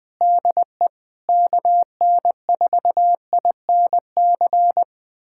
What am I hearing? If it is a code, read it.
Morse code, heard as a keyed tone, about 21 wpm, 700 Hz. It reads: DE KN4INC
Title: USA Callsign DE Callsign - 6 Characters - 20wpm